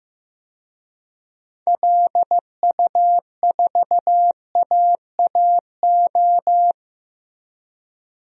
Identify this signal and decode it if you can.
Morse code, heard as a keyed tone, about 15 wpm, 700 Hz. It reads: LU4AAO
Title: DXCC-100 Call Signs - 6 Characters - Repeated 3x - 15wpm